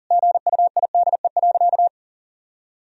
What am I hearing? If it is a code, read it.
Morse code, heard as a keyed tone, about 40 wpm, 700 Hz. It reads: GUIDE.
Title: Sentences from Top 1000 Words - Review Part 3 - 40wpm